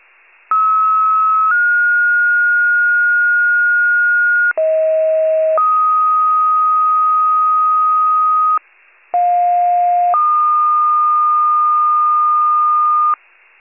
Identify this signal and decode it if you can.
Two-tone page, heard as a one-second tone, then a three-second tone, about 1290 then 1430 Hz, about 640 then 1250 Hz, about 710 then 1220 Hz. It is heard from 0.5 to 13.1 s.
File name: pager/84 FIRE